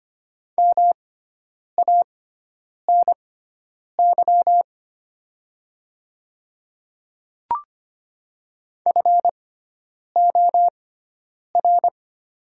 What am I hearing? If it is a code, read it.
Morse code, heard as a keyed tone, about 25 wpm, spaced as for 9 wpm, 700 Hz. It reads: MANY FOR
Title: 6x Character Spacing - Top 100 Words - 25wpm